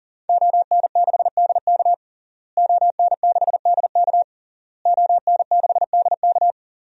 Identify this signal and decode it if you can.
Morse code, heard as a keyed tone, about 40 wpm, 700 Hz. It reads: ON6DK ON6DK ON6DK
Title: DXCC-100 Call Signs - 5 Characters - Repeated 3x - 40wpm